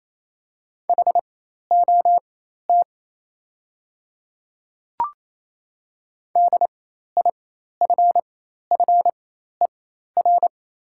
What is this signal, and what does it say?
Morse code, heard as a keyed tone, about 27 wpm, spaced as for 13 wpm, 690 Hz. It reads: HOT DIFFER
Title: 4x Character Spacing - Top 200 Words - 28wpm